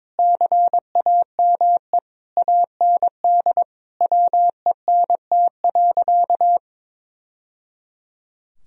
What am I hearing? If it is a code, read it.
Morse code, heard as a keyed tone, about 22 wpm, 700 Hz. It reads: CAME AND WENT.
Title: Sentences from Top 1000 Words - Review Part 2 - 22wpm